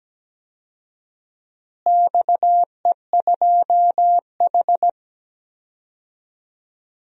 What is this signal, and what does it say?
Morse code, heard as a keyed tone, about 17 wpm, 700 Hz. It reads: XE2H
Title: DXCC-100 Call Signs - 4 Characters - Repeated 2x - 17wpm